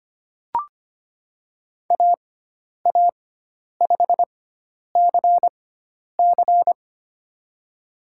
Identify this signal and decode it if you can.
Morse code, heard as a keyed tone, about 25 wpm, spaced as for 10 wpm, 700 Hz. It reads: AA5CC
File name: Lesson 16g - Letter c - Call Signs 25wpm